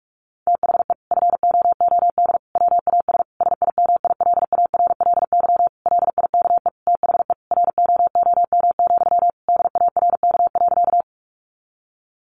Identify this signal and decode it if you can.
Morse code, heard as a keyed tone, about 44 wpm, 700 Hz. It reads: THE FOOD WAS SIMILARLY LIKE THE ROOM, DARK.
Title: Sentences from Top 1400 Words 45wpm